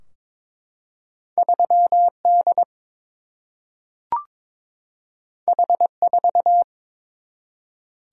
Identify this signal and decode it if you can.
Morse code, heard as a keyed tone, about 22 wpm, 700 Hz. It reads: 3 D H 4
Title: Single-Letter Single-Number 22wpm